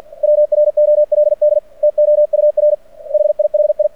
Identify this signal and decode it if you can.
Morse code, heard as a keyed tone, about 48 wpm, 600 Hz. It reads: WROLD TOUR HIHI
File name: archived / CRSnet